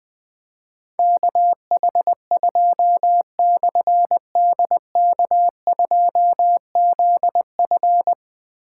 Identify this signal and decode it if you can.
Morse code, heard as a keyed tone, about 21 wpm, 700 Hz. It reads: KH2/DK2ZF
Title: Difficult Call Signs 20wpm